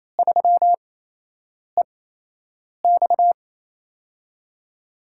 Morse code, heard as a keyed tone, about 28 wpm, spaced as for 8 wpm, 700 Hz. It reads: 3EX
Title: USA Call Signs - 8x Character Spacing - 28wpm